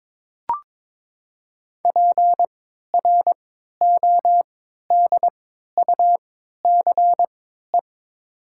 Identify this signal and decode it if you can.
Morse code, heard as a keyed tone, about 22 wpm, spaced as for 12 wpm, 700 Hz. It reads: PRODUCE
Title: 3x Character Spacing - Top 500 Words - 22wpm